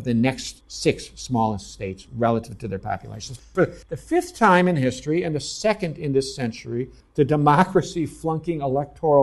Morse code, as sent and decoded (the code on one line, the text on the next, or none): none